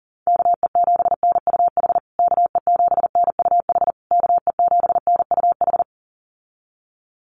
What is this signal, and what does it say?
Morse code, heard as a keyed tone, about 40 wpm, 700 Hz. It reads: KE7NUH KE7NUH KE7NUH
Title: USA Call Signs Repeated 3x 40wpm